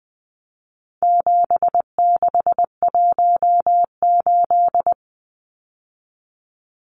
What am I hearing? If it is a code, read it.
Morse code, heard as a keyed tone, about 20 wpm, 700 Hz. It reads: 7618
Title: Numbers - 4-Digit 20wpm